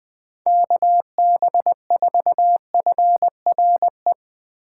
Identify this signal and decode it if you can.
Morse code, heard as a keyed tone, about 20 wpm, 700 Hz. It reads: KB4FRE
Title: USA Call Signs - 6 Characters - 2x Character Spacing - 20wpm